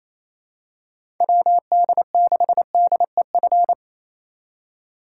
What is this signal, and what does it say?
Morse code, heard as a keyed tone, about 28 wpm, 700 Hz. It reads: WD6DEF
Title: USA Call Signs - 6 Characters - 4x Character Spacing - 28wpm